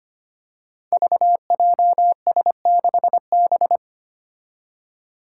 Morse code, heard as a keyed tone, about 25 wpm, 700 Hz. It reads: VJS6B
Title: Five Challenging Letter-Number Combinations 25wpm